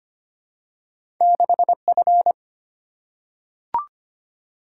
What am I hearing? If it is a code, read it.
Morse code, heard as a keyed tone, about 25 wpm, 700 Hz. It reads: 6F
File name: Two Letter-Number Combinations - 25wpm